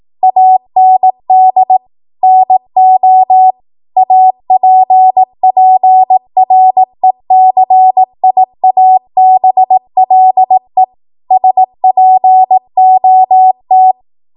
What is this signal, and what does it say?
Morse code, heard as a keyed tone, about 18 wpm, 750 Hz. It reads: AND NO APPRECIABLE SPOT
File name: Morse / 18WPM